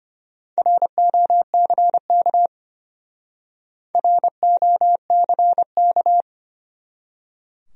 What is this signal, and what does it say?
Morse code, heard as a keyed tone, about 34 wpm, 700 Hz. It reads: ROCK ROCK